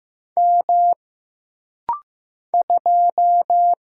Morse code, heard as a keyed tone, about 15 wpm, 700 Hz. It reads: M 2
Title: Single Letter-Number Alt - Rapid-Fire 15wpm